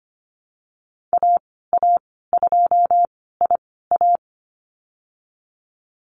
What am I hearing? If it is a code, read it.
Morse code, heard as a keyed tone, about 25 wpm, spaced as for 16 wpm, 700 Hz. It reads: AA2IA
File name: Lesson 15g - Number 5 - Call Signs 25wpm